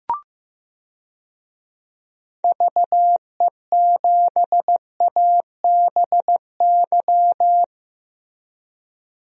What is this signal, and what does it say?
Morse code, heard as a keyed tone, about 15 wpm, 700 Hz. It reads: VE7ABY
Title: DXCC-100 Call Signs - 6 Characters - Repeated 2x - 15wpm